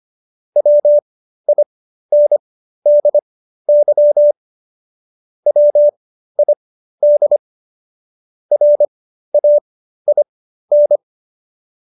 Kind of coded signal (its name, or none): Morse code